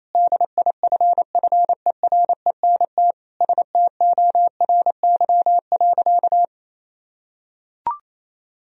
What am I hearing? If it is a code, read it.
Morse code, heard as a keyed tone, about 28 wpm, 700 Hz. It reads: DIFFERENT STORY.
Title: Sentences from Top 1000 Words - Review Part 3 - 28wpm